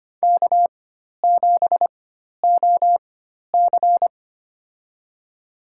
Morse code, heard as a keyed tone, about 25 wpm, spaced as for 12 wpm, 700 Hz. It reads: K7OC
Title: USA Call Signs - 4 Characters - 4x Character Spacing - 25wpm